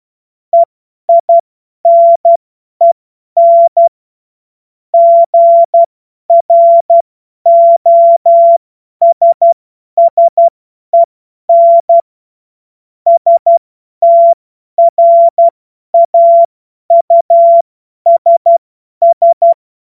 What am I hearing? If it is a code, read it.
Morse code, heard as a keyed tone, about 12 wpm, spaced as for 10 wpm, 680 Hz. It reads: EINEN GROSSEN STRAUSS
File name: Morsekurs / Witze / Witz 07. Juni 2022 50 F70